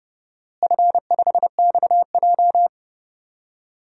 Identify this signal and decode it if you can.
Morse code, heard as a keyed tone, about 30 wpm, 700 Hz. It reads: F5XJ
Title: DXCC-100 Call Signs - 4 Characters - Repeated 2x - 30wpm